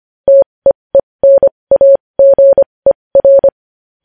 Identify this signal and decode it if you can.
Morse code, heard as a keyed tone, about 25 wpm, spaced as for 20 wpm, 550 Hz. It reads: TEENAGER
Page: News Headlines in Morse Code at 20 WPM